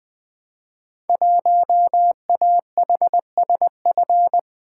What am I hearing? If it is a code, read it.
Morse code, heard as a keyed tone, about 21 wpm, 700 Hz. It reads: 1AHSF
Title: Code Groups - 5 Letter:Numbers - 20wpm